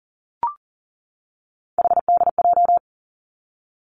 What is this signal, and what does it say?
Morse code, heard as a keyed tone, about 39 wpm, 700 Hz. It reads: HDJ
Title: Three Challenging Letter-Number Combinations 40wpm